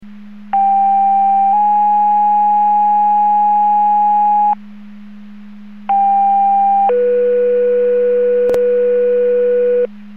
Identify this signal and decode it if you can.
Two-tone page, heard as a one-second tone, then a three-second tone, about 790 then 830 Hz, about 800 then 490 Hz. It is heard from 0.5 to 9.9 s.
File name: Portage / 00214 14-45 Portage 154.385 Paging